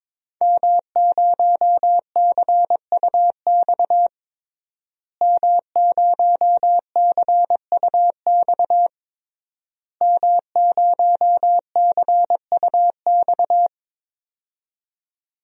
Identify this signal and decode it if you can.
Morse code, heard as a keyed tone, about 22 wpm, 700 Hz. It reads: M0CUX M0CUX M0CUX
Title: DXCC-100 Call Signs - 5 Characters - Repeated 3x - 22wpm